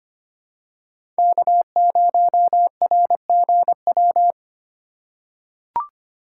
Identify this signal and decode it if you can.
Morse code, heard as a keyed tone, about 25 wpm, 700 Hz. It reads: K0RGW
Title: USA Call Signs - 5 Characters - 2x Character Spacing - 25wpm